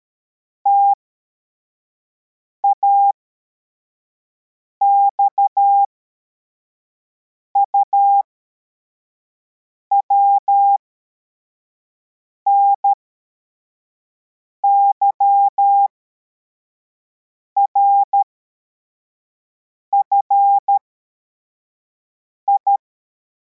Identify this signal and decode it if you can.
Morse code, heard as a keyed tone, about 13 wpm, spaced as for 4 wpm, 800 Hz. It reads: TAXUW NYRFI
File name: formation / cw / Text62